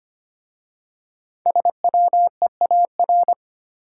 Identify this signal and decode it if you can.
Morse code, heard as a keyed tone, about 25 wpm, 700 Hz. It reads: SWEAR
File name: 5 Letter Words - First 3 Letters Given - 25wpm